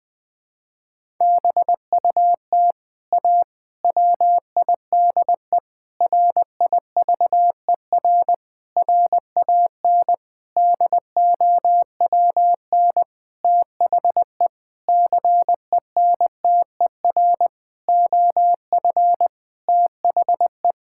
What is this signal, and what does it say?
Morse code, heard as a keyed tone, about 21 wpm, 700 Hz. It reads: BUT A WIDE RIVER RAN DOWN THE CENTER OF THE
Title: Sentences from Top 1000 Words - Review Part 6 - 20wpm